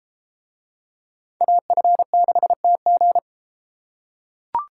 Morse code, heard as a keyed tone, about 32 wpm, 700 Hz. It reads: AF6TG